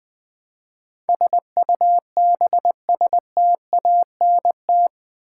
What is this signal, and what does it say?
Morse code, heard as a keyed tone, about 20 wpm, 700 Hz. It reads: SUBSTANT
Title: Build A Word - 4 Letters At A Time - 11-Letter Words - 20wpm